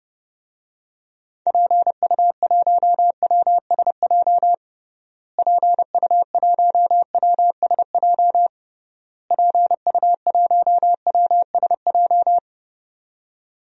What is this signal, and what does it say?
Morse code, heard as a keyed tone, about 32 wpm, 700 Hz. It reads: PU1WSJ PU1WSJ PU1WSJ
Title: DXCC-100 Call Signs - 6 Characters - Repeated 3x - 30wpm